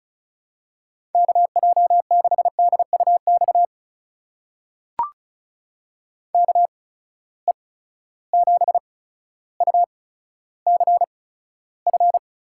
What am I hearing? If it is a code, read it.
Morse code, heard as a keyed tone, about 36 wpm, spaced as for 15 wpm, 700 Hz. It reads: KJ6DUX KE7UCF
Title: USA Call Signs - 6 Characters - 8x Character Spacing - 35wpm